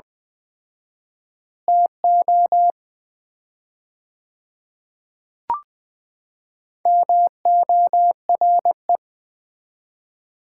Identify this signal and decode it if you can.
Morse code, heard as a keyed tone, about 20 wpm, 700 Hz. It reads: TO MORE
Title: 4x Word-Spacing - Sets of 4 Words - 20wpm